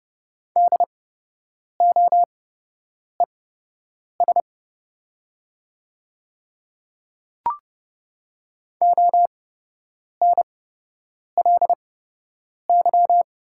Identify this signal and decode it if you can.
Morse code, heard as a keyed tone, about 31 wpm, spaced as for 8 wpm, 700 Hz. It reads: DOES ONLY
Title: 8x Character Spacing - Top 300 Words - 30wpm